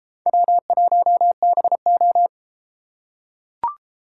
Morse code, heard as a keyed tone, about 31 wpm, 700 Hz. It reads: W1BO